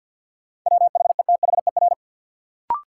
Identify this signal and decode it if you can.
Morse code, heard as a keyed tone, about 53 wpm, 700 Hz. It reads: WHETHER
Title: Sentences from Top 1000 Words - Review Part 3 - New Word Drill - 50wpm